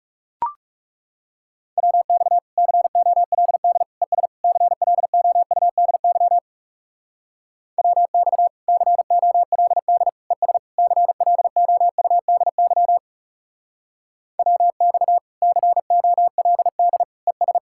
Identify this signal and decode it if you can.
Morse code, heard as a keyed tone, about 34 wpm, 700 Hz. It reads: WX COLD ES CLOUDY WX COLD ES CLOUDY WX COLD ES
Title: Speed-Racing Common QSO Phrases Encore - 30wpm